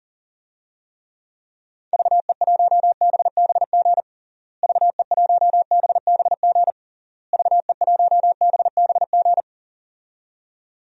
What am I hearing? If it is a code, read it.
Morse code, heard as a keyed tone, about 40 wpm, 700 Hz. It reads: VE1BBG VE1BBG VE1BBG
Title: DXCC-100 Call Signs - 6 Characters - Repeated 3x - 40wpm